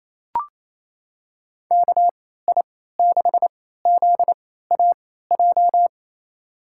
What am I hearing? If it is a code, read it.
Morse code, heard as a keyed tone, about 28 wpm, spaced as for 16 wpm, 700 Hz. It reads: KI6ZAJ
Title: USA Call Signs - 6 Characters - 3x Character Spacing - 28wpm